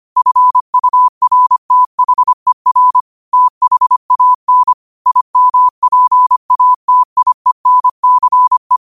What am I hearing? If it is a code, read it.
Morse code, heard as a keyed tone, about 25 wpm, 1000 Hz. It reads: FURTHER THAN IMPATIENCE